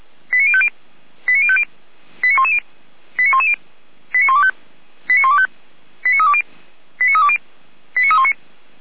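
Five-tone selective call (ZVEI), heard as quick runs of five tones, tones of about 70 ms. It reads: 89050 89050 89110 89110 89125 89125 89320 89320 89319
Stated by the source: ZVEI five-tone selective call received by SDR.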